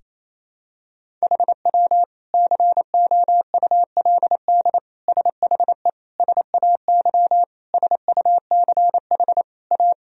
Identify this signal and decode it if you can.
Morse code, heard as a keyed tone, about 28 wpm, 700 Hz. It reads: HW COULD SHE SAY SUCH A
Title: Sentences from Top 200 Words and Abbreviations 28wpm